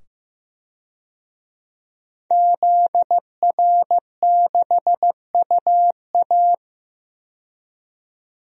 Morse code, heard as a keyed tone, about 15 wpm, 700 Hz. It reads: ZR6UA
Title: DXCC-100 Call Signs - 5 Characters - 15wpm